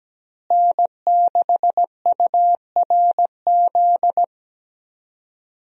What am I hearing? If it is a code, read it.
Morse code, heard as a keyed tone, about 17 wpm, 700 Hz. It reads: N6URZ
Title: USA Call Signs - 5 Characters - Repeated 2x - 17wpm